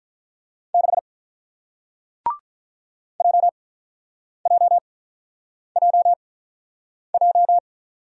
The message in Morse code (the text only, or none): 6 JJJJ